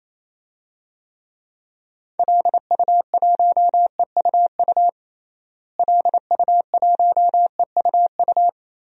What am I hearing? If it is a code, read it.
Morse code, heard as a keyed tone, about 28 wpm, 700 Hz. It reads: LU1EUU LU1EUU
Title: DXCC-100 Call Signs - 6 Characters - Repeated 2x - 28wpm